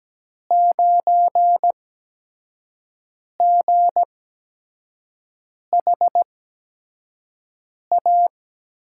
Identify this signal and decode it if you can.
Morse code, heard as a keyed tone, about 17 wpm, spaced as for 5 wpm, 700 Hz. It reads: 9GHA
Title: USA Call Signs - 5 Characters - 8x Character Spacing - 17wpm